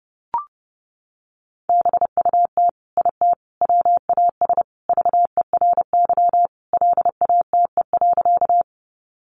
Morse code, heard as a keyed tone, about 30 wpm, 700 Hz. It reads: BUT IT WAS VERY LATE.
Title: Sentences from Top 300 Words 30wpm